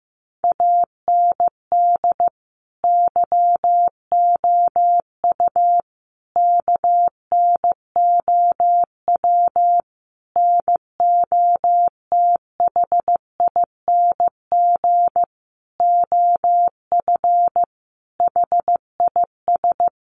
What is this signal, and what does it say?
Morse code, heard as a keyed tone, about 14 wpm, 700 Hz. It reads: AND YOU KNOW NOTHING OF HIS
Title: Sentences from Top 1200 Words 15wpm